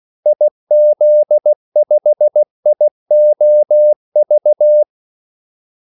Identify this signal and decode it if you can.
Morse code, heard as a keyed tone, about 16 wpm, 600 Hz. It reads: IZ5IOV
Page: Kaldesignaler 16 wpm